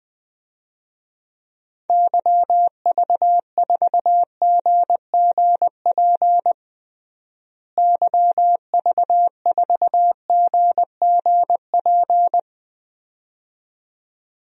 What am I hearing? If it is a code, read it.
Morse code, heard as a keyed tone, about 20 wpm, 700 Hz. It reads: YV4GGP YV4GGP
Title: DXCC-100 Call Signs - 6 Characters - Repeated 2x - 20wpm